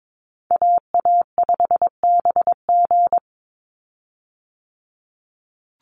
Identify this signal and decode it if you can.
Morse code, heard as a keyed tone, about 22 wpm, 700 Hz. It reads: AA5BG
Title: Call Signs Repeated 3x - 22wpm